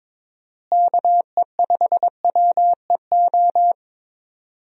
Morse code, heard as a keyed tone, about 22 wpm, 700 Hz. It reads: KE5WEO
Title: USA Call Signs - 6 Characters - Repeated 2x - 22wpm